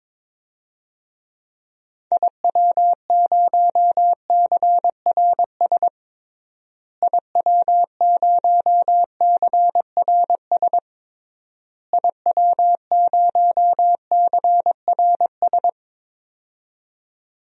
Morse code, heard as a keyed tone, about 21 wpm, 700 Hz. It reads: IW0CRS IW0CRS IW0CRS
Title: DXCC-100 Call Signs - 6 Characters - Repeated 3x - 22wpm